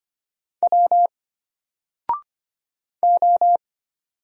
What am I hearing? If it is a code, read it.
Morse code, heard as a keyed tone, about 25 wpm, 700 Hz. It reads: W O
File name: Single-Letter-Number-25wpm